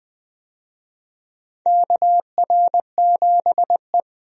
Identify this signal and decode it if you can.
Morse code, heard as a keyed tone, about 20 wpm, 700 Hz. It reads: KR7E
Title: USA Call Signs - 4 Characters - 20wpm